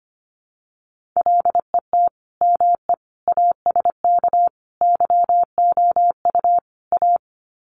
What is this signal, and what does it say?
Morse code, heard as a keyed tone, about 25 wpm, 700 Hz. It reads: LET ME ASK YOU A
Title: Sets of 5 Words Encore 25wpm